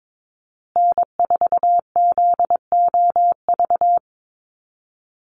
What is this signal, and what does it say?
Morse code, heard as a keyed tone, about 22 wpm, 700 Hz. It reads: N4ZOV
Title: USA Call Signs Repeated 3x 22wpm